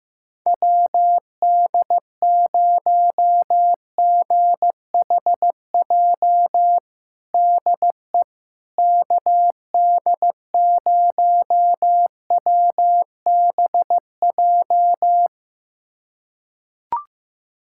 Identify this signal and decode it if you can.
Morse code, heard as a keyed tone, about 15 wpm, 700 Hz. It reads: WD0GHJ DE KD0WBJ
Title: USA Callsign DE Callsign - 6 Characters - 15wpm